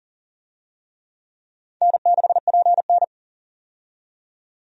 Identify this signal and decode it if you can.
Morse code, heard as a keyed tone, about 38 wpm, 700 Hz. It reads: N6PN
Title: USA Call Signs - 4 Characters - 40wpm